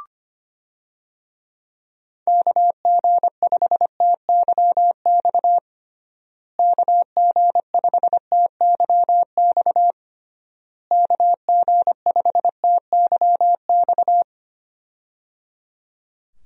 Morse code, heard as a keyed tone, about 25 wpm, 700 Hz. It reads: KG5TYX KG5TYX KG5TYX
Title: USA Call Signs - 6 Characters - Repeated 3x - 25wpm